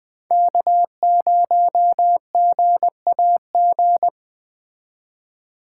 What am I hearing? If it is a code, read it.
Morse code, heard as a keyed tone, about 21 wpm, 700 Hz. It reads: K0GAG